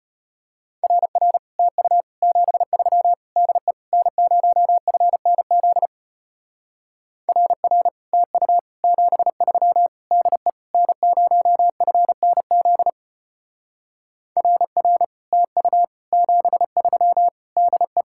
Morse code, heard as a keyed tone, about 33 wpm, 700 Hz. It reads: RR TU 73 DE N0FNZ RR TU 73 DE N0FNZ RR TU 73 DE